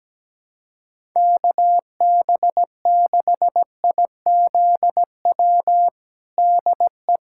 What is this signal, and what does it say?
Morse code, heard as a keyed tone, about 17 wpm, 700 Hz. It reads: KB6IZW DE
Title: USA Callsign DE Callsign 17wpm